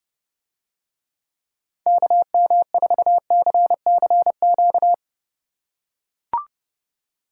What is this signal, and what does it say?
Morse code, heard as a keyed tone, about 30 wpm, 700 Hz. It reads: KM4CCQ